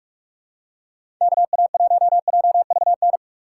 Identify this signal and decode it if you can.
Morse code, heard as a keyed tone, about 46 wpm, 700 Hz. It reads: KA1JVN
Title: USA Call Signs - 4x Character Spacing - 45wpm